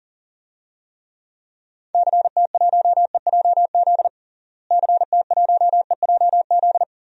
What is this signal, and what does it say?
Morse code, heard as a keyed tone, about 40 wpm, 700 Hz. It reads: CT1EJZ CT1EJZ
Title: DXCC-100 Call Signs - 6 Characters - Repeated 2x - 40wpm